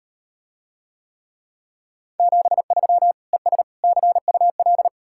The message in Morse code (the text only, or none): 73 ES CUL